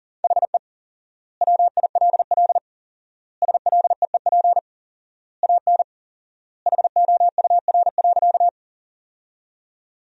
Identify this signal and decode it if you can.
Morse code, heard as a keyed tone, about 40 wpm, 700 Hz. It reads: HE WILL SLEEP AN HOUR.
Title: 4x Word-Spacing - Sentences from Top 500 Words - 40wpm